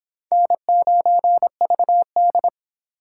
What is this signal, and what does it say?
Morse code, heard as a keyed tone, about 22 wpm, 700 Hz. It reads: N9VD